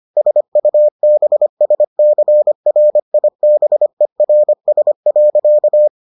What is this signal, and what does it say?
Morse code, heard as a keyed tone, about 25 wpm, 600 Hz. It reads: SUBSCRIBERS.